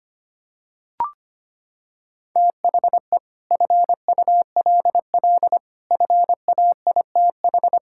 Morse code, heard as a keyed tone, about 25 wpm, 700 Hz. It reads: THE FULL FAITH
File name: Sets of 9 Words - 25wpm